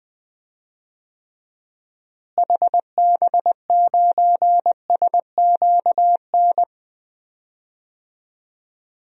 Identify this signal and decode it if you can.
Morse code, heard as a keyed tone, about 20 wpm, 700 Hz. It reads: HB9SQN
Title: DXCC-100 Call Signs - 6 Characters - Repeated 2x - 20wpm